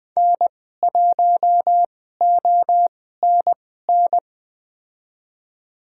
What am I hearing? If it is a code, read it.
Morse code, heard as a keyed tone, about 20 wpm, spaced as for 15 wpm, 700 Hz. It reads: N1ONN
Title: USA Call Signs - 5 Characters - 2x Character Spacing - 20wpm